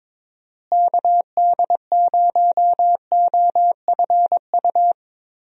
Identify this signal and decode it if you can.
Morse code, heard as a keyed tone, about 22 wpm, 700 Hz. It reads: KD0OFU
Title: USA Call Signs - 6 Characters - Repeated 2x - 22wpm